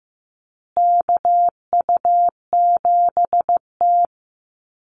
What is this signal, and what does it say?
Morse code, heard as a keyed tone, about 15 wpm, 700 Hz. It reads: KU7T
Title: Call Signs Repeated 3x - 15wpm